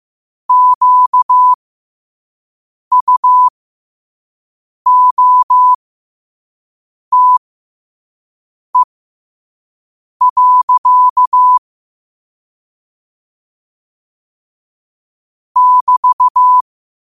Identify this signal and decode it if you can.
Morse code, heard as a keyed tone, about 15 wpm, spaced as for 5 wpm, 1000 Hz. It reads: QUOTE. <BT>
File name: qotdfiles / 5wpm